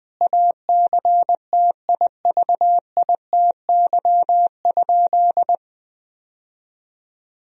Morse code, heard as a keyed tone, about 20 wpm, 700 Hz. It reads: ACTIVITY?
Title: Sentences from Top 1000 Words - Review Part 4 - 20wpm